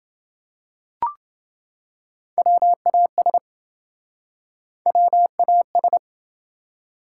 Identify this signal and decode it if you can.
Morse code, heard as a keyed tone, about 27 wpm, 700 Hz. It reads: WAS WAS